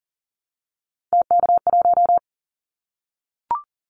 Morse code, heard as a keyed tone, about 40 wpm, 700 Hz. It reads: TK1
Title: Callsign Prefixes for Top 100 DXCC - 40wpm